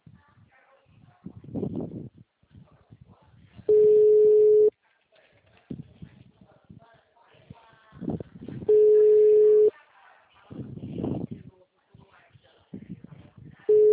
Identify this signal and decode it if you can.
Ringback tone (Europe, 425 Hz), with speech.